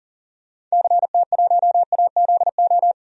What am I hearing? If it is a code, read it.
Morse code, heard as a keyed tone, about 40 wpm, 700 Hz. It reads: CT1AZO
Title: DXCC-100 Call Signs - 6 Characters - 40wpm